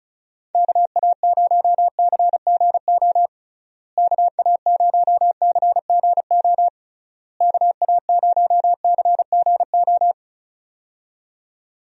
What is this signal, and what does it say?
Morse code, heard as a keyed tone, about 36 wpm, 700 Hz. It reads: KA0CGO KA0CGO KA0CGO
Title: USA Call Signs - 6 Characters - Repeated 3x - 35wpm